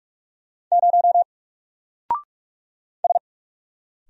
Morse code, heard as a keyed tone, about 44 wpm, 700 Hz. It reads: 0 S
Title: Single Letter-Number 45wpm